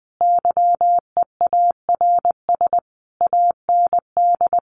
Morse code, heard as a keyed tone, about 20 wpm, 700 Hz. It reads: YEARS AND
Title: A Sound Of Thunder 20wpm